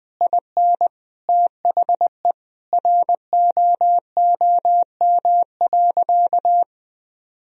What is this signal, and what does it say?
Morse code, heard as a keyed tone, about 21 wpm, 700 Hz. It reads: IN THE ROOM.
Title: Sentences from Top 1000 Words - Review Part 4 - 20wpm